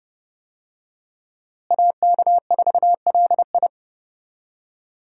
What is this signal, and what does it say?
Morse code, heard as a keyed tone, about 30 wpm, 700 Hz. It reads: AK4LI
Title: USA Call Signs - 5 Characters - Repeated 3x - 30wpm